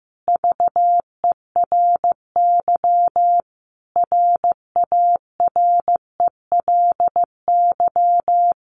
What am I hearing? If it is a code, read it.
Morse code, heard as a keyed tone, about 15 wpm, 700 Hz. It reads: VERY RARELY
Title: Sentences from Top 1400 Words 15wpm